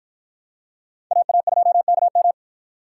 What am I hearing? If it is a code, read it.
Morse code, heard as a keyed tone, about 51 wpm, 700 Hz. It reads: AA2KM